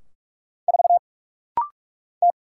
Morse code, heard as a keyed tone, about 46 wpm, 700 Hz. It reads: 4 T
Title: Single Letter-Number Alt - Rapid-Fire 45wpm